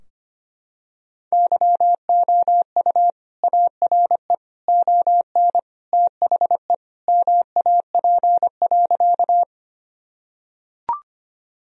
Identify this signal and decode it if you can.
Morse code, heard as a keyed tone, about 25 wpm, 700 Hz. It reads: YOU ARE ON THE MAP.
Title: Sentences from Top 1000 Words - Review Part 2 - 25wpm